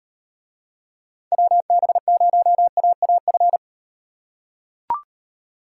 Morse code, heard as a keyed tone, about 37 wpm, 700 Hz. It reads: WB0AAF